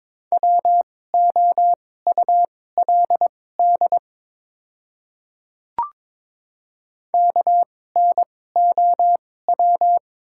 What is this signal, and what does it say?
Morse code, heard as a keyed tone, about 22 wpm, spaced as for 16 wpm, 690 Hz. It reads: WOULD KNOW
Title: 2x Character Spacing - Top 100 Words - 22wpm